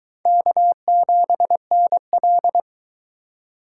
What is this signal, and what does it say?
Morse code, heard as a keyed tone, about 22 wpm, 700 Hz. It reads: K7NL